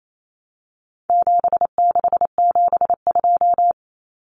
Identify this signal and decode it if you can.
Morse code, heard as a keyed tone, about 28 wpm, 700 Hz. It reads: 7672
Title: Common Cut Numbers - 4-Digit 28wpm